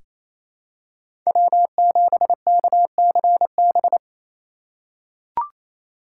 Morse code, heard as a keyed tone, about 28 wpm, 700 Hz. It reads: W7KCB
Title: USA Call Signs - 3x Character Spacing - 28wpm